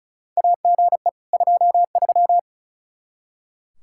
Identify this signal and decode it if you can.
Morse code, heard as a keyed tone, about 36 wpm, 700 Hz. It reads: AGE 23
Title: Practice QSO - Exchange 3 - More Info - 35wpm